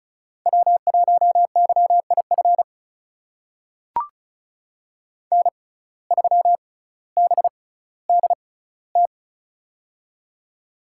Morse code, heard as a keyed tone, about 36 wpm, spaced as for 18 wpm, 700 Hz. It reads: W1YIF N3BDT
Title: USA Call Signs - 5 Characters - 6x Character Spacing - 35wpm